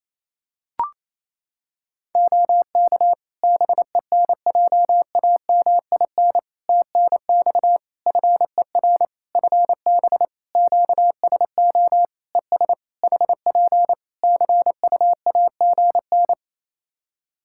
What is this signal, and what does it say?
Morse code, heard as a keyed tone, about 27 wpm, 700 Hz. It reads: OK BENJAMIN TNX FER FB QSO ES HP CUAGN
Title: Practice QSO - Exchange 4 - Ending - 28wpm